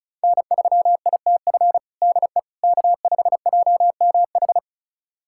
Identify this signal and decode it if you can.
Morse code, heard as a keyed tone, about 36 wpm, 700 Hz. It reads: N3ITF DE K5JMH
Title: USA Callsign DE Callsign - 5 Characters - 35wpm